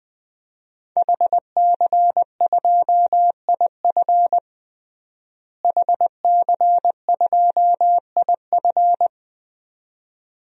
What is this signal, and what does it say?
Morse code, heard as a keyed tone, about 20 wpm, 700 Hz. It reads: HC2IF HC2IF
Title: DXCC-100 Call Signs - 5 Characters - Repeated 2x - 20wpm